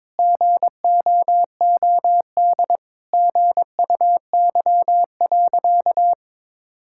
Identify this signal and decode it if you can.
Morse code, heard as a keyed tone, about 22 wpm, 700 Hz. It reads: GOOD GUY.
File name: practice / 22wpm / Sentences with 3-Word Phrases - 22wpm